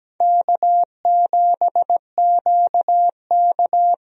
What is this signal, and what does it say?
Morse code, heard as a keyed tone, about 17 wpm, 700 Hz. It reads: K7QK
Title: USA Call Signs - 4 Characters - Repeated 2x - 17wpm